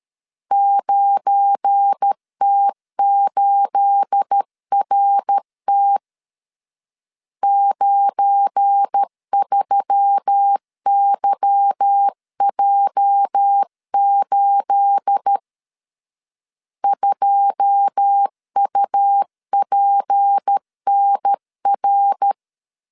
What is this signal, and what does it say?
Morse code, heard as a keyed tone, about 13 wpm, 800 Hz. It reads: 9T8RT 93YJ8 2UPNR
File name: LEZIONI / 9 28 12wpm 2351